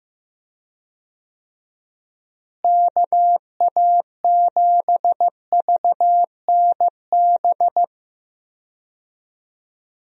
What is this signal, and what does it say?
Morse code, heard as a keyed tone, about 15 wpm, 700 Hz. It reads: KA7VNB
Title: USA Call Signs - 6 Characters - Repeated 3x - 15wpm